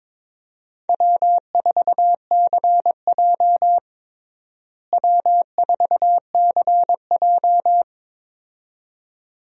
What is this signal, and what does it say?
Morse code, heard as a keyed tone, about 22 wpm, 700 Hz. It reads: W4CJ W4CJ
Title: USA Call Signs - 4 Characters - Repeated 2x - 22wpm